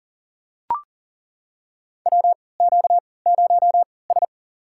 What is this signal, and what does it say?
Morse code, heard as a keyed tone, about 40 wpm, spaced as for 23 wpm, 700 Hz. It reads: WQ0S
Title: USA Call Signs - 3x Character Spacing - 40wpm